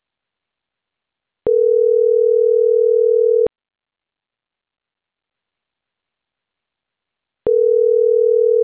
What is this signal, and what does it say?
Ringback tone (North America), 440 + 480 Hz.